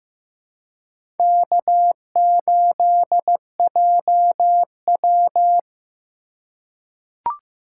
Morse code, heard as a keyed tone, about 15 wpm, 700 Hz. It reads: K8JW
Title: USA Call Signs - 4 Characters - 2x Character Spacing - 15wpm